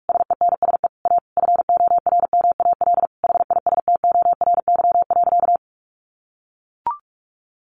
Morse code, heard as a keyed tone, about 44 wpm, 700 Hz. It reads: SENSE A FORMAL HISTORY.
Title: Sentences from Top 1300 Words 45wpm